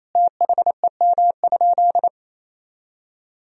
Morse code, heard as a keyed tone, about 28 wpm, 700 Hz. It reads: THEM?
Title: Sentences from Top 1000 Words - Review Part 2 - 28wpm